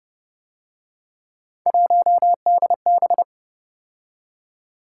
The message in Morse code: 1DB

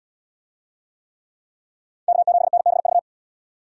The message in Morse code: K7NBV

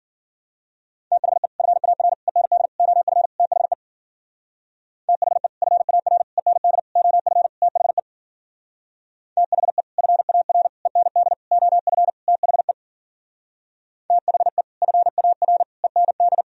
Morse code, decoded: THE FAR END OF THE THE FAR END OF THE THE FAR END OF THE THE FAR END

